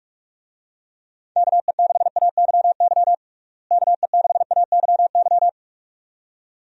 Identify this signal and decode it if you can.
Morse code, heard as a keyed tone, about 43 wpm, 700 Hz. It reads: KE6AYY KE6AYY
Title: USA Call Signs - 6 Characters - Repeated 2x - 45wpm